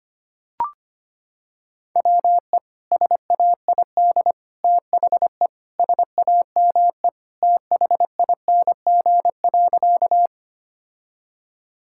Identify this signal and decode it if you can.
Morse code, heard as a keyed tone, about 25 wpm, 700 Hz. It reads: WE SAID THE SAME THING.
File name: practice / 25wpm / Sentences with 3-Word Phrases - 25wpm